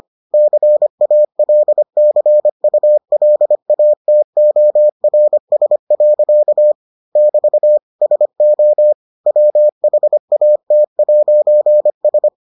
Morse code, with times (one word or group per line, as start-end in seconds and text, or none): 0.3-6.7 CALCULATORS.
7.2-7.8 <BT>
8.0-8.9 SO
9.3-12.3 WHAT'S